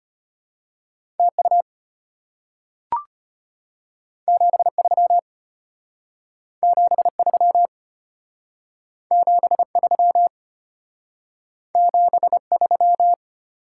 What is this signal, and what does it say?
Morse code, heard as a keyed tone, about 34 wpm, 700 Hz. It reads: TU 73 73 73 73